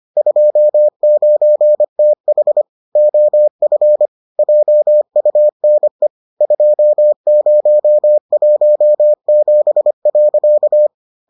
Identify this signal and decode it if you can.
Morse code, heard as a keyed tone, about 25 wpm, 600 Hz. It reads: 29TH OF JUNE 2017.